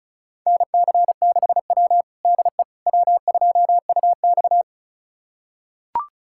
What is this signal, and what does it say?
Morse code, heard as a keyed tone, about 36 wpm, 700 Hz. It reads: NC6W DE W2UX
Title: USA Callsign DE Callsign - 4 Characters - 35wpm